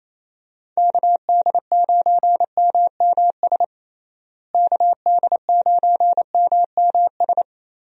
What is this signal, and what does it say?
Morse code, heard as a keyed tone, about 28 wpm, 700 Hz. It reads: KD9MMS KD9MMS
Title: USA Call Signs - 6 Characters - Repeated 3x - 28wpm